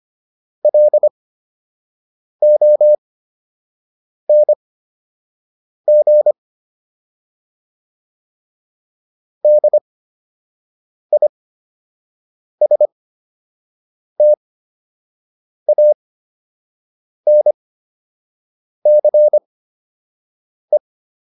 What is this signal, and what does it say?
Morse code, heard as a keyed tone, about 25 wpm, spaced as for 6 wpm, 600 Hz. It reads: LONG DISTANCE